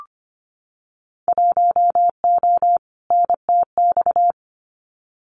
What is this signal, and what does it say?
Morse code, heard as a keyed tone, about 25 wpm, 700 Hz. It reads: 1O NTX
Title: Winter Field Day Exchanges 25wpm